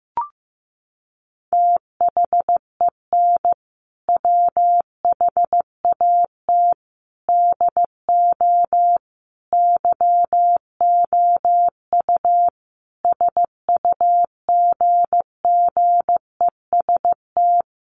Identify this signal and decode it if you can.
Morse code, heard as a keyed tone, about 15 wpm, 700 Hz. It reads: THEN WHAT DO YOU SUGGEST
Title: Sentences from Top 1300 Words 15wpm